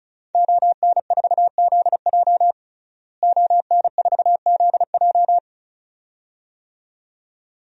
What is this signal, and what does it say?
Morse code, heard as a keyed tone, about 36 wpm, 700 Hz. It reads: ON4ZJ ON4ZJ
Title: DXCC-100 Call Signs - 5 Characters - Repeated 2x - 35wpm